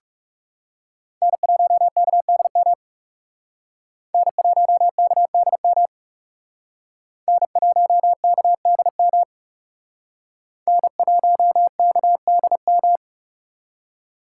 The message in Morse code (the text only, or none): N1KDM N1KDM N1KDM N1KDM